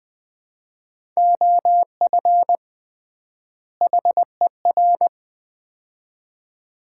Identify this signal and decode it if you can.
Morse code, heard as a keyed tone, about 20 wpm, 700 Hz. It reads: OF HER
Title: 3x Word-Spacing - Sets of 2 Words - 20wpm